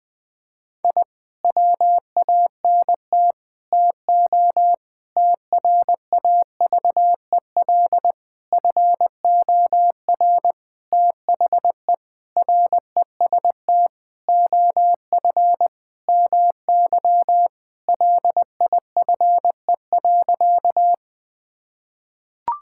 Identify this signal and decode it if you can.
Morse code, heard as a keyed tone, about 20 wpm, 700 Hz. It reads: I WANT TO TRAVEL FOR THE REST OF MY LIFE.
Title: Sentences with 5-Word Phrases - 20wpm